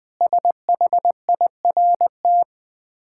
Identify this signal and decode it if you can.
Morse code, heard as a keyed tone, about 20 wpm, 700 Hz. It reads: SHIRT
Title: Difficult Words with Hints - 20wpm